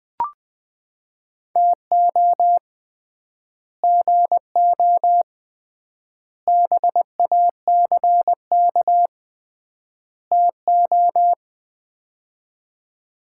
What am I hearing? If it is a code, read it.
Morse code, heard as a keyed tone, about 20 wpm, 700 Hz. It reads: TO GO BACK TO
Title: 3x Word-Spacing - Sets of 4 Words - 20wpm